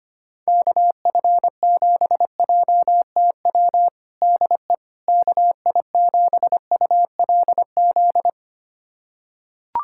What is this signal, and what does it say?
Morse code, heard as a keyed tone, about 25 wpm, 700 Hz. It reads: KF7JTW DE KI7ULZ
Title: USA Callsign DE Callsign 25wpm